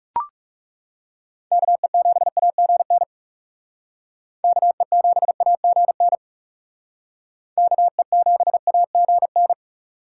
Morse code, heard as a keyed tone, about 39 wpm, 700 Hz. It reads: KE7AGN KE7AGN KE7AGN